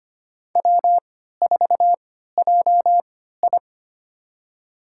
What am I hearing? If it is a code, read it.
Morse code, heard as a keyed tone, about 25 wpm, spaced as for 14 wpm, 700 Hz. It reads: W4JI
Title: USA Call Signs - 4 Characters - 3x Character Spacing - 25wpm